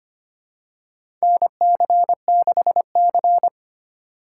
Morse code, heard as a keyed tone, about 25 wpm, 700 Hz. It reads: NC6C
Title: USA Call Signs - 4 Characters - 2x Character Spacing - 25wpm